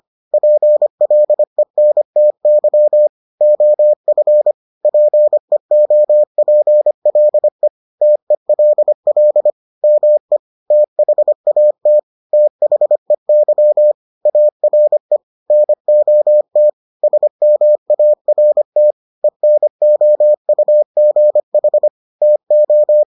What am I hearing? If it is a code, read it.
Morse code, heard as a keyed tone, about 25 wpm, 600 Hz. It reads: PLENTY OF PEOPLE TELL ME THAT THEY ARE NOT SMART ENOUGH TO